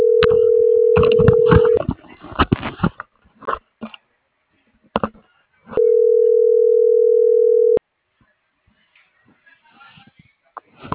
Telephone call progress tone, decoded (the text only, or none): ringback tone